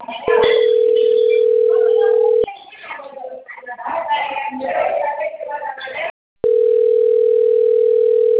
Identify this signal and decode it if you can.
Ringback tone (North America), 440 + 480 Hz, with speech.
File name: out- - - - - .18797